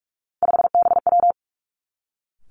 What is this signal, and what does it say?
Morse code, heard as a keyed tone, about 46 wpm, 700 Hz. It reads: 5BW
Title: Three Challenging Letter-Number Combinations 45wpm